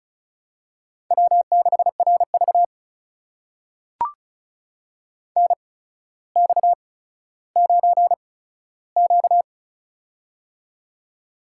Morse code, heard as a keyed tone, about 36 wpm, spaced as for 15 wpm, 700 Hz. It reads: W6RV NX9Q
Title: USA Call Signs - 4 Characters - 8x Character Spacing - 35wpm